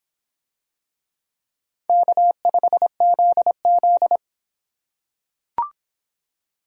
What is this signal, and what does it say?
Morse code, heard as a keyed tone, about 25 wpm, 700 Hz. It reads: K5ZZ